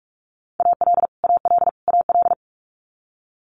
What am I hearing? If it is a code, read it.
Morse code, heard as a keyed tone, about 46 wpm, 700 Hz. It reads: AL AL AL
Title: Multi-letter Phonemes Repeated 3x 45wpm